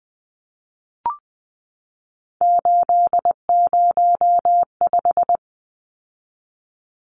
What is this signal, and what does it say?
Morse code, heard as a keyed tone, about 20 wpm, 700 Hz. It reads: 805
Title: Numbers - 3-Digit 20wpm